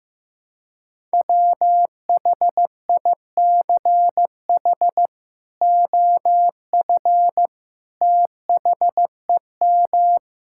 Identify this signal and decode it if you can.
Morse code, heard as a keyed tone, about 15 wpm, 700 Hz. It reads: WHICH OF THEM
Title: Sentences from Top 1600 Words 15wpm